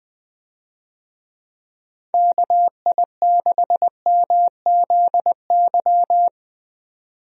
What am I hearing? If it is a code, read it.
Morse code, heard as a keyed tone, about 21 wpm, 700 Hz. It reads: KI6MZY
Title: USA Call Signs - 6 Characters - 20wpm